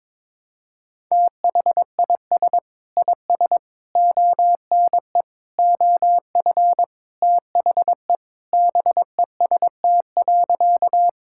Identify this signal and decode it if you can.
Morse code, heard as a keyed tone, about 22 wpm, 700 Hz. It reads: THIS IS ONE OF THE BEST.
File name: practice / 22wpm / Sentences with 4-Word Phrases - 22wpm